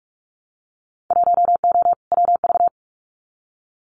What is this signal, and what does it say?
Morse code, heard as a keyed tone, about 46 wpm, 700 Hz. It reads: 1O WV
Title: Winter Field Day Exchanges Encore 45wpm